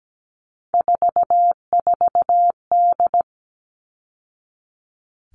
Morse code, heard as a keyed tone, about 17 wpm, 700 Hz. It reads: 44D
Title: All Cut Numbers - 3-Digit 17wpm